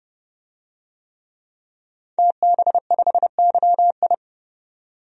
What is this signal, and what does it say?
Morse code, heard as a keyed tone, about 32 wpm, 700 Hz. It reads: TB5YI
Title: DXCC-100 Call Signs - 5 Characters - 30wpm